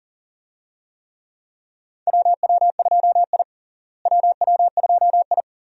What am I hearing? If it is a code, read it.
Morse code, heard as a keyed tone, about 40 wpm, 700 Hz. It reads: WW2I WW2I
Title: USA Call Signs - 4 Characters - Repeated 2x - 40wpm